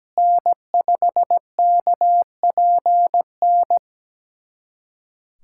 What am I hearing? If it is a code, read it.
Morse code, heard as a keyed tone, about 17 wpm, 700 Hz. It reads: N5KPN
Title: USA Callsign DE Callsign - 5 Characters - 17wpm